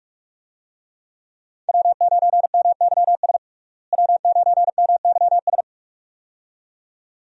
Morse code, heard as a keyed tone, about 41 wpm, 700 Hz. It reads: W9MYS W9MYS
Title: USA Call Signs - 5 Characters - Repeated 2x - 45wpm